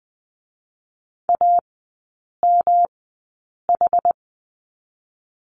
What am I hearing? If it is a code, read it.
Morse code, heard as a keyed tone, about 20 wpm, 700 Hz. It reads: A M H
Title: 107 Lesson 18e - Letter m - Three Characters 20wpm